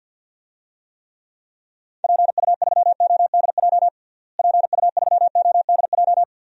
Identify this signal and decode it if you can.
Morse code, heard as a keyed tone, about 52 wpm, 700 Hz. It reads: PU3ODJ PU3ODJ
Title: DXCC-100 Call Signs - 6 Characters - Repeated 3x - 50wpm